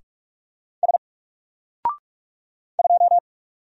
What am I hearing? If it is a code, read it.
Morse code, heard as a keyed tone, about 44 wpm, 700 Hz. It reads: S 2
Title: Single Character (Letter:Number:Punctuation:Prosign) Alt 45wpm